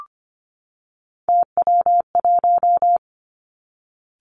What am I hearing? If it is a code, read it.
Morse code, heard as a keyed tone, about 25 wpm, 700 Hz. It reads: TW1
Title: Three Challenging Letter-Number Combinations 25wpm